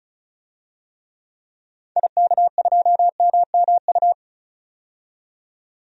Morse code, heard as a keyed tone, about 35 wpm, 700 Hz. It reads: IK2MMU
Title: DXCC-100 Call Signs - 6 Characters - 35wpm